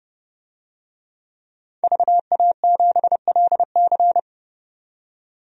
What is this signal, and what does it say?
Morse code, heard as a keyed tone, about 30 wpm, 700 Hz. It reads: VA7LC
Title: DXCC-100 Call Signs - 5 Characters - 30wpm